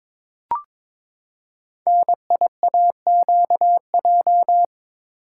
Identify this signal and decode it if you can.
Morse code, heard as a keyed tone, about 21 wpm, 700 Hz. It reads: NIAQJ
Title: Code Groups - 5 Letters - 22wpm